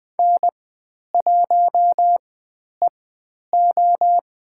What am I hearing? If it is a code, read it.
Morse code, heard as a keyed tone, about 20 wpm, spaced as for 10 wpm, 700 Hz. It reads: N1EO
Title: 109 Lesson 18g - Letter m - Call Signs 20wpm